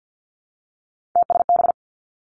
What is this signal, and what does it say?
Morse code, heard as a keyed tone, about 50 wpm, 700 Hz. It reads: TSB
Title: Three Challenging Letter-Number Combinations 50wpm